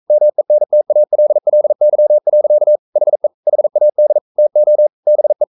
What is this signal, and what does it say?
Morse code, heard as a keyed tone, about 42 wpm, 600 Hz. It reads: MENTALLY. HE HAD TO BE
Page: Romanen Never af Ken Follet på engelsk.